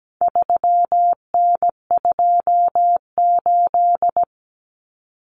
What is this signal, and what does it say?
Morse code, heard as a keyed tone, about 17 wpm, 700 Hz. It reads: 3N28
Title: Common Cut Numbers - 4-Digit 17wpm